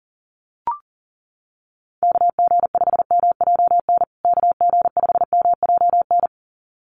Morse code, heard as a keyed tone, about 40 wpm, 700 Hz. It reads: KG5MJN KG5MJN
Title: USA Call Signs Repeated 2x 40wpm